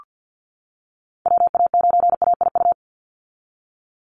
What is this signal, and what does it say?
Morse code, heard as a keyed tone, about 48 wpm, 700 Hz. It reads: WA9AIU
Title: USA Call Signs 50wpm